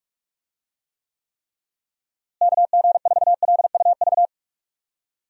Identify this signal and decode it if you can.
Morse code, heard as a keyed tone, about 46 wpm, 700 Hz. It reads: KG4LUV
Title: USA Call Signs - 6 Characters - Repeated 3x - 45wpm